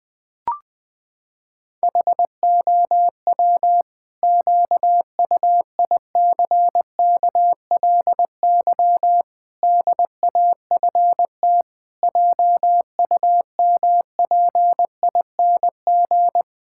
Morse code, heard as a keyed tone, about 20 wpm, 700 Hz. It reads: HOW QUICKLY DAFT JUMPING
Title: Pangram Challenge v2 - 20wpm